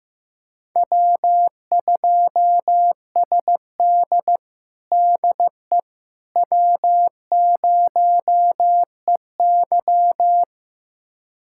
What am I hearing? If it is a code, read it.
Morse code, heard as a keyed tone, about 15 wpm, 700 Hz. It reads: W2SD DE W0EY
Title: USA Callsign DE Callsign - 4 Characters - 15wpm